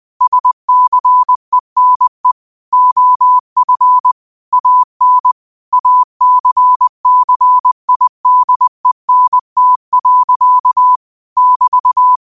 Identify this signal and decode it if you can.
Morse code, heard as a keyed tone, about 20 wpm, 1000 Hz. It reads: SCENE OF AN ACCIDENT. <BT>